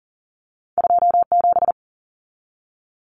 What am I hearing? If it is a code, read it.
Morse code, heard as a keyed tone, about 40 wpm, 700 Hz. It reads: 27
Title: Common Cut Numbers - 2-Digit 40wpm